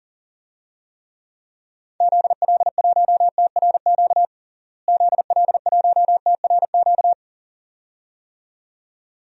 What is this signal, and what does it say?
Morse code, heard as a keyed tone, about 40 wpm, 700 Hz. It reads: ZL1TRQ ZL1TRQ
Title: DXCC-100 Call Signs - 6 Characters - Repeated 2x - 40wpm